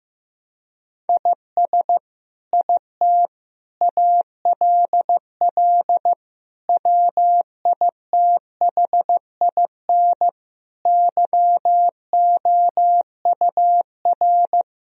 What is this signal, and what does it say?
Morse code, heard as a keyed tone, about 15 wpm, 700 Hz. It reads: IS IT ALL WITHIN YOUR
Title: Sentences from Top 1000 Words - Review Part 2 - 15wpm